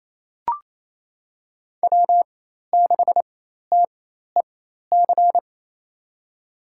Morse code, heard as a keyed tone, about 28 wpm, spaced as for 13 wpm, 700 Hz. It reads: W6TEC
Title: USA Call Signs - 5 Characters - 4x Character Spacing - 28wpm